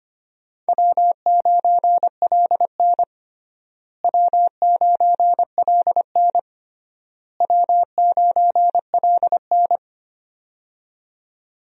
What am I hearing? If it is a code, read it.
Morse code, heard as a keyed tone, about 25 wpm, 700 Hz. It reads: W9LN W9LN W9LN
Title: USA Call Signs - 4 Characters - Repeated 3x - 25wpm